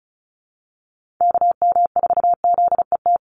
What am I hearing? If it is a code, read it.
Morse code, heard as a keyed tone, about 34 wpm, 700 Hz. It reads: KM4ZET
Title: USA Call Signs 35wpm